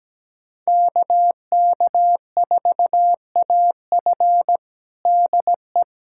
Morse code, heard as a keyed tone, about 17 wpm, 700 Hz. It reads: KK4AF DE
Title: USA Callsign DE Callsign - 5 Characters - 17wpm